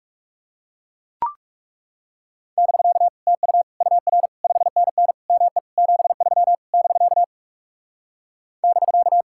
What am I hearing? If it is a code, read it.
Morse code, heard as a keyed tone, about 39 wpm, 700 Hz. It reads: <BK> TU UR 5NN ME 73 <BK> <BK>